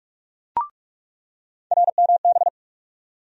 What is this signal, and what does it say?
Morse code, heard as a keyed tone, about 43 wpm, 700 Hz. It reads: RMB
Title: USA Call Sign Postfixes - 3 Letters 45wpm